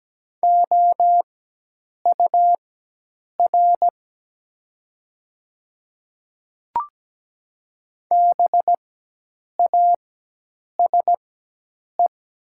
Morse code, heard as a keyed tone, about 17 wpm, spaced as for 8 wpm, 700 Hz. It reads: OUR BASE
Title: 4x Character Spacing - Top 500 Words - 17wpm